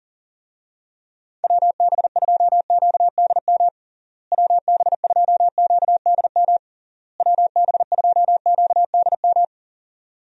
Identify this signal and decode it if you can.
Morse code, heard as a keyed tone, about 37 wpm, 700 Hz. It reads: WB2QDM WB2QDM WB2QDM
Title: USA Call Signs - 6 Characters - Repeated 3x - 40wpm